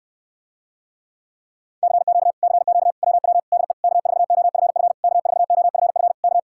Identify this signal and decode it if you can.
Morse code, heard as a keyed tone, about 69 wpm, 700 Hz. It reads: CQ CQ WW DE K4OFF K4OFF K